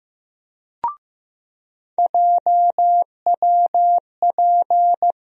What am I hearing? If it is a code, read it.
Morse code, heard as a keyed tone, about 15 wpm, 700 Hz. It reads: JWP
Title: USA Call Sign Postfixes - 3 Letters 15wpm